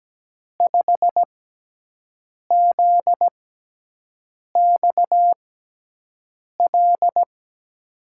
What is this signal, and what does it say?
Morse code, heard as a keyed tone, about 17 wpm, spaced as for 6 wpm, 700 Hz. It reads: 5ZXL
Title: USA Call Signs - 6 Characters - 6x Character Spacing - 17wpm